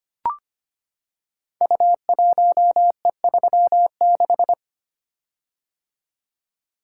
Morse code, heard as a keyed tone, about 25 wpm, 700 Hz. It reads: U1E36
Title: Code Groups - 5 Letter:Numbers - 25wpm